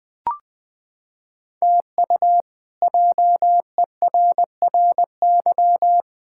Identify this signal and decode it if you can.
Morse code, heard as a keyed tone, about 20 wpm, 700 Hz. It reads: TU JERRY
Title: K1USN SST QSO Practice - 20wpm